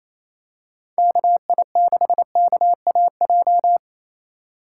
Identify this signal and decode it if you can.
Morse code, heard as a keyed tone, about 28 wpm, 700 Hz. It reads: KI6KAJ
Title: USA Call Signs - 6 Characters - 4x Character Spacing - 28wpm